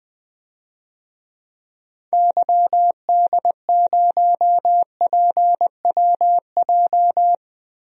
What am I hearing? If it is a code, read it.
Morse code, heard as a keyed tone, about 20 wpm, 700 Hz. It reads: YD0PWJ
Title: DXCC-100 Call Signs - 6 Characters - 20wpm